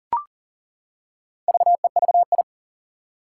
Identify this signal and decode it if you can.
Morse code, heard as a keyed tone, about 37 wpm, 700 Hz. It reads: VEVI
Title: Four Challenging Letter-Number Combinations 40wpm